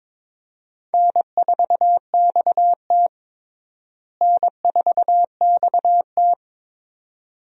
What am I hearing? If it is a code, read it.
Morse code, heard as a keyed tone, about 22 wpm, 700 Hz. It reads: N4XT N4XT
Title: USA Call Signs - 4 Characters - Repeated 2x - 22wpm